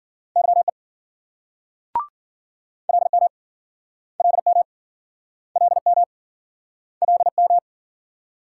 Morse code, heard as a keyed tone, about 53 wpm, 700 Hz. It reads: YE LM LM LM LM